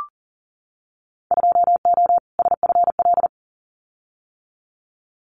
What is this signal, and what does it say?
Morse code, heard as a keyed tone, about 40 wpm, 700 Hz. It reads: 2O SFL
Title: Winter Field Day Exchanges 40wpm